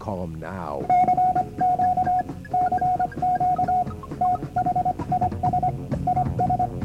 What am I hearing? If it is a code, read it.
Morse code, heard as a keyed tone, about 25 wpm, 730 Hz, with speech. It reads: CQ CQ THIS IS